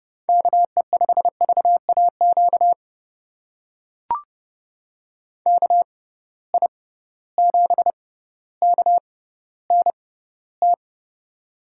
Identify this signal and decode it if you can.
Morse code, heard as a keyed tone, about 30 wpm, spaced as for 15 wpm, 700 Hz. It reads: KE5VAQ KI7KNT
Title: USA Call Signs - 6 Characters - 6x Character Spacing - 30wpm